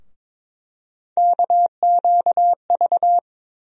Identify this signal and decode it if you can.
Morse code, heard as a keyed tone, about 23 wpm, 700 Hz. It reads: KQV
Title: USA Call Sign Postfixes - 3 Letters 22wpm